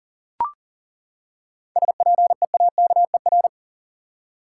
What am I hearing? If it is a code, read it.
Morse code, heard as a keyed tone, about 39 wpm, 700 Hz. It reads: SPEAKER
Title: Seven Letter Words - 40wpm